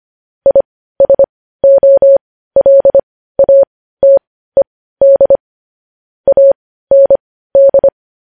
Morse code, heard as a keyed tone, about 25 wpm, spaced as for 15 wpm, 550 Hz. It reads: ISOLATED AND
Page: News Headlines in Morse Code at 15 WPM